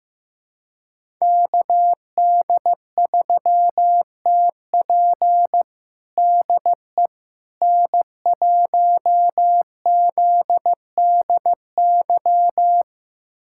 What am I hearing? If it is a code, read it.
Morse code, heard as a keyed tone, about 15 wpm, 700 Hz. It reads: KD3TP DE N1ZDY
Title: USA Callsign DE Callsign - 5 Characters - 15wpm